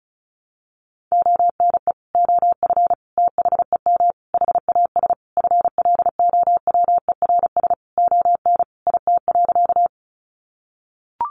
Morse code, heard as a keyed tone, about 34 wpm, 700 Hz. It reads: ONE OF THEM HAS FLOWERS ON IT.
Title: Additional Easy Phrases 35wpm